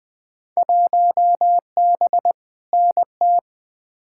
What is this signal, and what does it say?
Morse code, heard as a keyed tone, about 20 wpm, 700 Hz. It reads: 1B NT
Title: Field Day Exchanges Encore - 20wpm